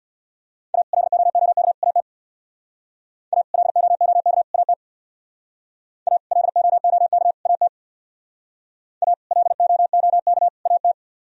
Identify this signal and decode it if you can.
Morse code, heard as a keyed tone, about 63 wpm, 700 Hz. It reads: A LOOK AT A LOOK AT A LOOK AT A LOOK AT